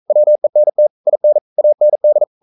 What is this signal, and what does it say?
Morse code, heard as a keyed tone, about 42 wpm, 600 Hz. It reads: WENT IN AND
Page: Romanen Never af Ken Follet på engelsk.